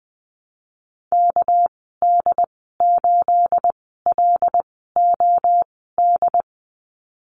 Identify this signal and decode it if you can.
Morse code, heard as a keyed tone, about 20 wpm, spaced as for 15 wpm, 700 Hz. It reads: KD8LOD
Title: 231 Lesson 37g - Letter x - Call Signs 20wpm